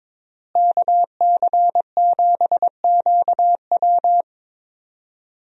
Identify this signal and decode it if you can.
Morse code, heard as a keyed tone, about 21 wpm, 700 Hz. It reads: KC7QW
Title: USA Call Signs - 5 Characters - Repeated 3x - 22wpm